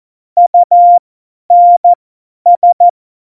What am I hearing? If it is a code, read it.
Morse code, heard as a keyed tone, about 14 wpm, spaced as for 10 wpm, 700 Hz. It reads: UNS
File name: Morsekurs / Witze / Witz 21. Mai 2019, 50 F70